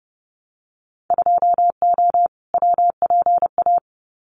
Morse code, heard as a keyed tone, about 30 wpm, 700 Hz. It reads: 2O WPA
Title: Winter Field Day Exchanges 30wpm